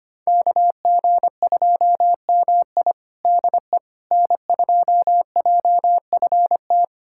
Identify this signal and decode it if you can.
Morse code, heard as a keyed tone, about 25 wpm, 700 Hz. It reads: KG2MI DE N2JFT
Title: USA Callsign DE Callsign 25wpm